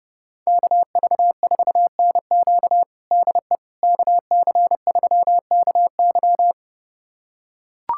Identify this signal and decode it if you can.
Morse code, heard as a keyed tone, about 30 wpm, 690 Hz. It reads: KV4NQ DE KC3KY
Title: USA Callsign DE Callsign 30wpm